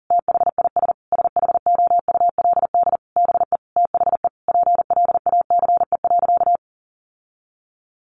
Morse code, heard as a keyed tone, about 40 wpm, 700 Hz. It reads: THIS SHOULD BE THE PLACE.
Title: Sentences from Top 1300 Words 40wpm